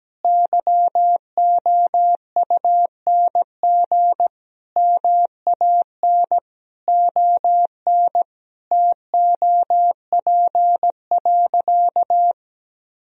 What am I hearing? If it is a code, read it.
Morse code, heard as a keyed tone, about 17 wpm, 700 Hz. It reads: YOUNG MAN ON TOP.
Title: Sentences from Top 1000 Words - Review Part 2 - 17wpm